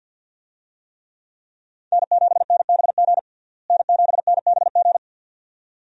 Morse code, heard as a keyed tone, about 51 wpm, 700 Hz. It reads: N7NBG N7NBG
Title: USA Call Signs - 5 Characters - Repeated 2x - 50wpm